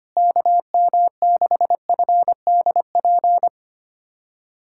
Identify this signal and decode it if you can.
Morse code, heard as a keyed tone, about 25 wpm, 700 Hz. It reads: KM6FDP
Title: USA Callsign DE Callsign - 6 Characters - 25wpm